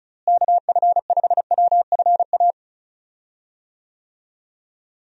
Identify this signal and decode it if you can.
Morse code, heard as a keyed tone, about 34 wpm, 700 Hz. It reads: KF5WFA
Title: USA Call Signs - 6 Characters - Repeated 2x - 35wpm